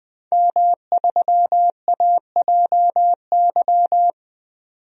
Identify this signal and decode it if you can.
Morse code, heard as a keyed tone, about 21 wpm, 700 Hz. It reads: M3AJY
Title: DXCC-100 Call Signs - 5 Characters - 20wpm